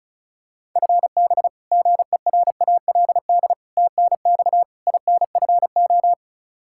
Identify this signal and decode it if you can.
Morse code, heard as a keyed tone, about 34 wpm, 700 Hz. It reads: FB GERALD TNX INFO
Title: Practice QSO - Exchange 3 - More Info - 35wpm